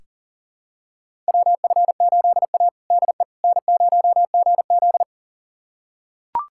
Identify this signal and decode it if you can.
Morse code, heard as a keyed tone, about 40 wpm, 700 Hz. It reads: WF8A DE N0GZ
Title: USA Callsign DE Callsign - 4 Characters - 40wpm